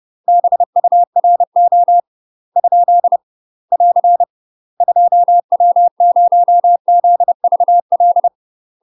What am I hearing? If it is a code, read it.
Morse code, heard as a keyed tone, about 30 wpm, 700 Hz. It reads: BURO ? <AR> 2W0ZVL
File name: UK MASTER QSOs TRAINING / 30WPM / UK MASTER QSOs 0027